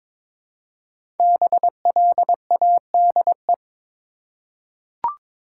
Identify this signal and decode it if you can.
Morse code, heard as a keyed tone, about 23 wpm, 700 Hz. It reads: BLADE
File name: 5 Letter Words - First 3 Letters Given - 22wpm